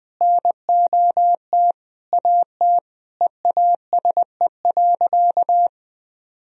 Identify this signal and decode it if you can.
Morse code, heard as a keyed tone, about 20 wpm, 700 Hz. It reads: NOT AT EASE.
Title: Sentences from Top 1000 Words - Review Part 1 - 20wpm